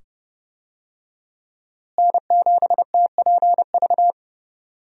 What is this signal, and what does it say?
Morse code, heard as a keyed tone, about 30 wpm, 700 Hz. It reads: N7TPV
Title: USA Call Signs - 5 Characters - Repeated 2x - 30wpm